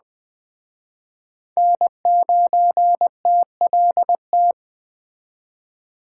Morse code, heard as a keyed tone, about 20 wpm, 700 Hz. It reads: N9TLT
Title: Speed Racing - USA 1x3 Call Signs 17wpm